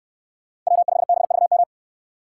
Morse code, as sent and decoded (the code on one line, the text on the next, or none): .-- ..... -... ...- --
W5BVM